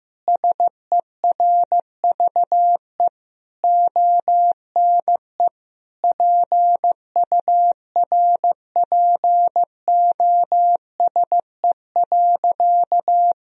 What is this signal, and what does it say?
Morse code, heard as a keyed tone, about 16 wpm, 700 Hz. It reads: SERVE ONE PURPOSE.
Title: Sentences from Top 1000 Words - Review Part 2 - 15wpm